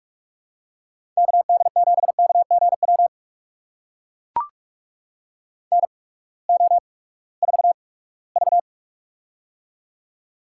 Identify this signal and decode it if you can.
Morse code, heard as a keyed tone, about 44 wpm, spaced as for 22 wpm, 700 Hz. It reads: KD7KGW NO4V
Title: USA Call Signs - 8x Character Spacing - 45wpm